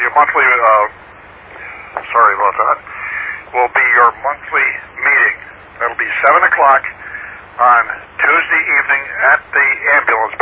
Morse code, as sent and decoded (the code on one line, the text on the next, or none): none